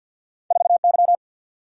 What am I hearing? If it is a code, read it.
Morse code, heard as a keyed tone, about 50 wpm, 700 Hz. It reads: 4 Y
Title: Single-Letter Single-Number 50wpm